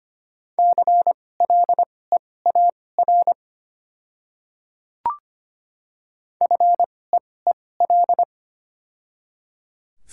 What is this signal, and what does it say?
Morse code, heard as a keyed tone, about 25 wpm, spaced as for 18 wpm, 700 Hz. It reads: CLEAR FEEL
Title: 2x Character Spacing - Top 500 Words - 25wpm